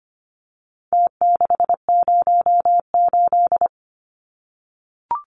Morse code, heard as a keyed tone, about 25 wpm, 700 Hz. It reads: T608
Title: Numbers - Contest Style - 4-Digit 25wpm